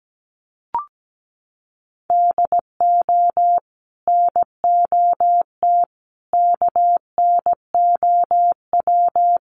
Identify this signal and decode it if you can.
Morse code, heard as a keyed tone, about 17 wpm, 700 Hz. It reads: DO NOT KNOW
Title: Sets of 5 Words 17wpm